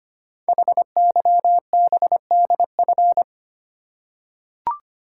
Morse code, heard as a keyed tone, about 25 wpm, 700 Hz. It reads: HYBDF
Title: Code Groups - 5 Letter:Numbers - 25wpm